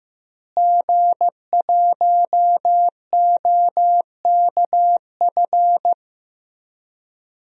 Morse code, heard as a keyed tone, about 15 wpm, 700 Hz. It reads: G1OKF
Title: DXCC-100 Call Signs - 5 Characters - Repeated 2x - 15wpm